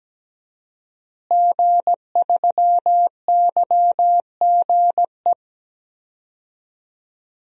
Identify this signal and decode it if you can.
Morse code, heard as a keyed tone, about 17 wpm, 700 Hz. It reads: G3YGE
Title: DXCC-100 Call Signs - 5 Characters - Repeated 2x - 17wpm